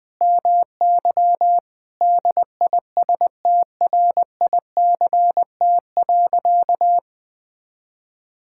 Morse code, heard as a keyed tone, about 21 wpm, 700 Hz. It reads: MY DISTRICT.
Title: Sentences from Top 1600 Words 20wpm